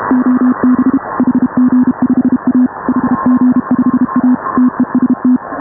Morse code, heard as a keyed tone, about 32 wpm, 250 Hz. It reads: OB HG5A HG5A TEST
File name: AUDIO2019 / HA5BA CW